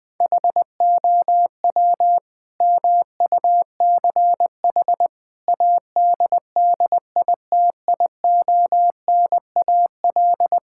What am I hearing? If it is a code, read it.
Morse code, heard as a keyed tone, about 20 wpm, 700 Hz. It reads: HOW MUCH ADDITIONAL
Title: Sentences from Top 1000 Words - Review Part 5 - 20wpm